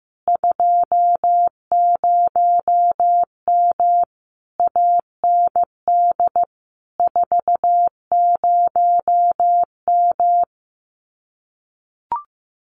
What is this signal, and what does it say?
Morse code, heard as a keyed tone, about 15 wpm, 700 Hz. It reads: 20M AND 40M
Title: Common QSO Phrases Encore 15wpm